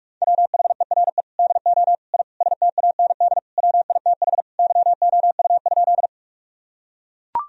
Morse code, heard as a keyed tone, about 41 wpm, 700 Hz. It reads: WHERE DO I STAND WITH YOU?
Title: Sentences from Top 1000 Words - Review Part 1 - 45wpm